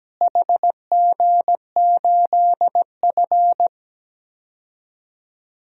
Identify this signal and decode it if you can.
Morse code, heard as a keyed tone, about 17 wpm, 700 Hz. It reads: HG8F
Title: DXCC-100 Call Signs - 4 Characters - Repeated 2x - 17wpm